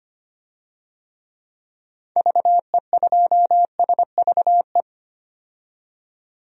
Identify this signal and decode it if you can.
Morse code, heard as a keyed tone, about 25 wpm, 700 Hz. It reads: VE2SVE
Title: DXCC-100 Call Signs - 6 Characters - 25wpm